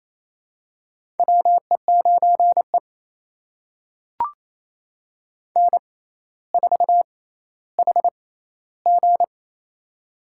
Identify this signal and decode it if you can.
Morse code, heard as a keyed tone, about 28 wpm, spaced as for 14 wpm, 700 Hz. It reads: WE9E N4HG
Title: USA Call Signs - 4 Characters - 6x Character Spacing - 28wpm